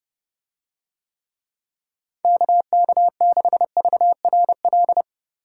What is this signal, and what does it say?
Morse code, heard as a keyed tone, about 29 wpm, 700 Hz. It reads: KK6VRL